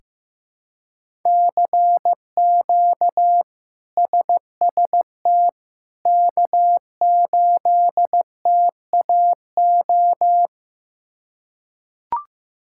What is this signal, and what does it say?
Morse code, heard as a keyed tone, about 15 wpm, 700 Hz. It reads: CQ SST K8TAO
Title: K1USN SST QSO Practice - 15wpm